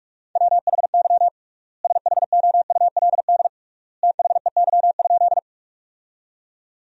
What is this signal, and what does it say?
Morse code, heard as a keyed tone, about 41 wpm, 700 Hz. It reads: WHY SHOULD THEY?
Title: 3x Word-Spacing - Sentences from Top 300 Words - 45wpm